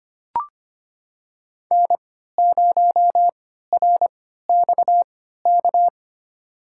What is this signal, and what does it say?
Morse code, heard as a keyed tone, about 25 wpm, spaced as for 14 wpm, 700 Hz. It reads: N0RXK
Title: USA Call Signs - 5 Characters - 3x Character Spacing - 25wpm